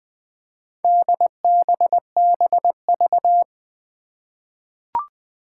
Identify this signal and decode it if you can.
Morse code, heard as a keyed tone, about 20 wpm, 700 Hz. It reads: DBBV
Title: Four Challenging Letter-Number Combinations 20wpm